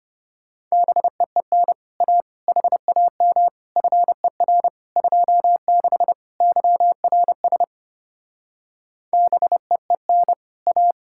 Morse code, heard as a keyed tone, about 33 wpm, 700 Hz. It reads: BEEN A HAM FER 26 YRS BEEN A